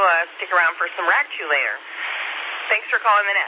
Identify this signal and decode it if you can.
Emergency Alert System (SAME) tones, heard as harsh buzzing bursts, with speech. It is heard from 0.5 to 3.0 s.